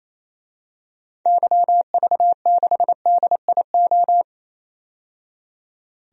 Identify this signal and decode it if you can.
Morse code, heard as a keyed tone, about 28 wpm, 700 Hz. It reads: YV6DIO
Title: DXCC-100 Call Signs - 6 Characters - 28wpm